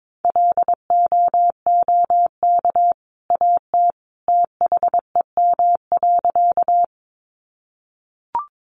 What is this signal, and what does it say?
Morse code, heard as a keyed tone, about 22 wpm, 700 Hz. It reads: LOOK AT THEM.
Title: A Sound Of Thunder 22wpm